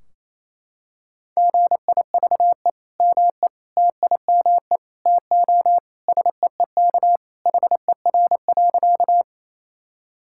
Morse code, heard as a keyed tone, about 28 wpm, 700 Hz. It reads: GIVE ME TIME TO SEEK HER.
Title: Sentences from Top 1600 Words 28wpm